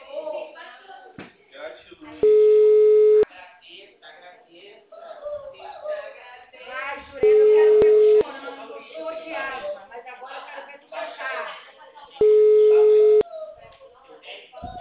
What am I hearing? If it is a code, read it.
Ringback tone (Europe, 425 Hz), with speech.